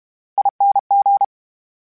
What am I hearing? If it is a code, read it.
Morse code, heard as a keyed tone, about 32 wpm, 800 Hz. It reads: ING
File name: Special INGs / Special INGs-mp3s / VerySlowToFaster Ing @30